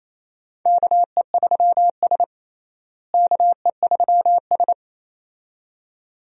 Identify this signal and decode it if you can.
Morse code, heard as a keyed tone, about 28 wpm, 700 Hz. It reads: KE3S KE3S
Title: USA Call Signs - 4 Characters - Repeated 2x - 28wpm